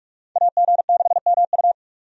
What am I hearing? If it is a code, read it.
Morse code, heard as a keyed tone, about 44 wpm, 700 Hz. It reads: AG6MU
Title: USA Call Signs - 5 Characters - 45wpm